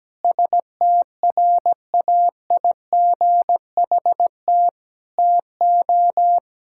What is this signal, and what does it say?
Morse code, heard as a keyed tone, about 17 wpm, 700 Hz. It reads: STRAIGHT TO
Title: Sentences from Top 1000 Words - Review Part 6 - 17wpm